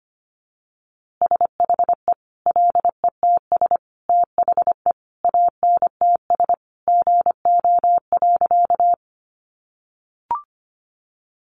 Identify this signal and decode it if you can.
Morse code, heard as a keyed tone, about 25 wpm, 700 Hz. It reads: SHE LETS THE ANTS GO.
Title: Easy Phrases 25wpm